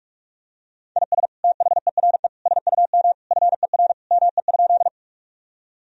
Morse code, heard as a keyed tone, about 46 wpm, 700 Hz. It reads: IS THERE SUM FER ME?
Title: Sentences from Top 200 Words and Abbreviations 45wpm